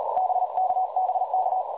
Morse code, heard as a keyed tone, about 42 wpm, 770 Hz, with noise. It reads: BK BK
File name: dlazaawansowanych / nagraniazpasma / websdr recording 3565.0kHz